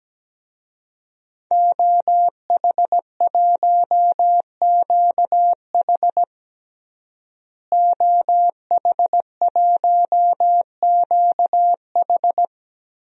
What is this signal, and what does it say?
Morse code, heard as a keyed tone, about 17 wpm, 700 Hz. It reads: OH1QH OH1QH
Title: DXCC-100 Call Signs - 5 Characters - Repeated 3x - 17wpm